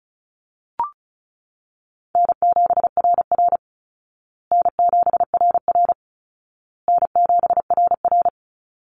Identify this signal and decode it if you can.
Morse code, heard as a keyed tone, about 34 wpm, 700 Hz. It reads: N7RR N7RR N7RR
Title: Call Signs Repeated 2x - 35wpm